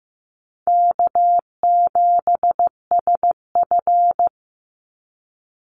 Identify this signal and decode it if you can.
Morse code, heard as a keyed tone, about 15 wpm, 700 Hz. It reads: K7SF
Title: Easy Call Signs Repeated 2x - 15wpm